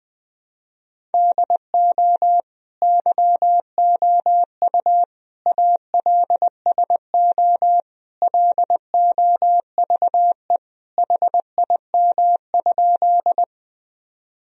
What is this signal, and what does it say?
Morse code, heard as a keyed tone, about 20 wpm, 700 Hz. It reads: DO YOU ALSO LOVE HIM?
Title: Sentences from Top 1000 Words - Review Part 1 - 20wpm